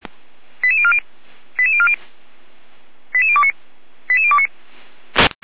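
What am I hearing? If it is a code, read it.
Five-tone selective call (ZVEI), heard as quick runs of five tones, tones of about 70 ms. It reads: 88040 88040 88029 88029
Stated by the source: ZVEI five-tone selective call received on SDR, from the feuerwehr-orlamuende.org fire brigade audio server.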